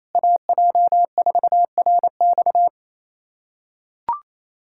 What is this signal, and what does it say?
Morse code, heard as a keyed tone, about 28 wpm, 700 Hz. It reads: AJ4RX
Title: USA Callsign DE Callsign - 5 Characters - 28wpm